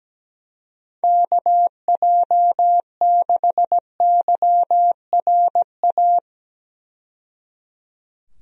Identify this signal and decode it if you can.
Morse code, heard as a keyed tone, about 17 wpm, 700 Hz. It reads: KJ6YRA
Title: USA Call Signs - 6 Characters - 17wpm